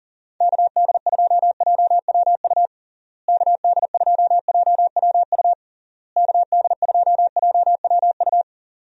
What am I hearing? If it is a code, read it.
Morse code, heard as a keyed tone, about 40 wpm, 700 Hz. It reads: KD2JWU KD2JWU KD2JWU
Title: USA Call Signs - 6 Characters - Repeated 3x - 40wpm